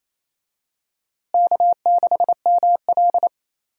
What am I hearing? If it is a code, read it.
Morse code, heard as a keyed tone, about 28 wpm, 700 Hz. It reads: K6ML
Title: USA Call Signs - 4 Characters - Repeated 2x - 28wpm